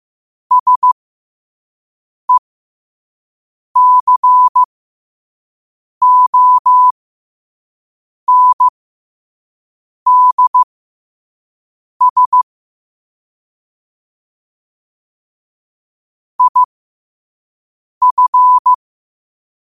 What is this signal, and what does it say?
Morse code, heard as a keyed tone, about 15 wpm, spaced as for 5 wpm, 1000 Hz. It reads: SECONDS IF
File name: qotdfiles / 5wpm